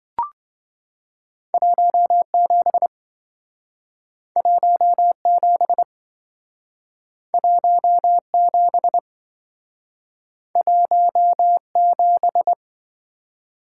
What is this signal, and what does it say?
Morse code, heard as a keyed tone, about 27 wpm, 700 Hz. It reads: 17 17 17 17